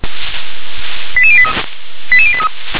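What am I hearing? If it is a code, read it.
Five-tone selective call (ZVEI), heard as quick runs of five tones, tones of about 70 ms. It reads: 88083 88083
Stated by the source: ZVEI five-tone selective call received on an SDR.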